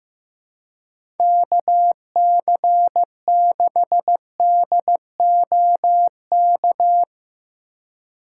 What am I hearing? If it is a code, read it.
Morse code, heard as a keyed tone, about 15 wpm, 700 Hz. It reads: KC6DOK
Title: USA Call Signs - 6 Characters - 3x Character Spacing - 15wpm